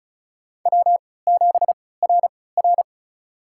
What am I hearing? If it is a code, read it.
Morse code, heard as a keyed tone, about 34 wpm, spaced as for 20 wpm, 700 Hz. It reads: W7RR
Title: USA Call Signs - 4 Characters - 3x Character Spacing - 35wpm